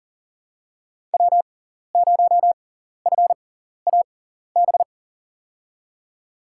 Morse code, heard as a keyed tone, about 40 wpm, spaced as for 14 wpm, 700 Hz. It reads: W0FAB
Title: USA Call Signs - 5 Characters - 6x Character Spacing - 40wpm